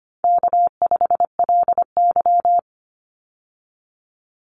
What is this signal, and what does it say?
Morse code, heard as a keyed tone, about 25 wpm, 700 Hz. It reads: K5LY
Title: Easy Call Signs Repeated 2x - 25wpm